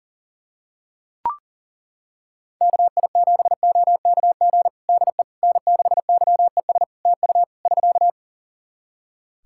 Morse code, heard as a keyed tone, about 40 wpm, 700 Hz. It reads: KI7OKG DE N6YES TU <SK>
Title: Practice QSO - Exchange 4 - Ending - 40wpm